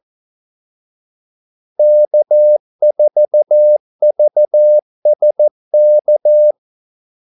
Morse code, heard as a keyed tone, about 14 wpm, 600 Hz. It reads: K4VSK
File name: Callsigns-14wpm